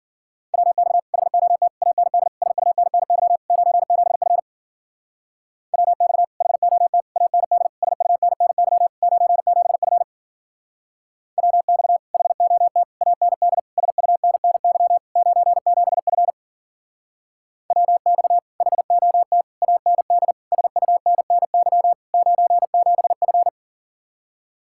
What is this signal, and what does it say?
Morse code, heard as a keyed tone, about 49 wpm, 700 Hz. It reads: WX HOT AND SUNNY 97F WX HOT AND SUNNY 97F WX HOT AND SUNNY 97F WX HOT AND SUNNY 97F
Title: Speed-Racing Common QSO Phrases - 40wpm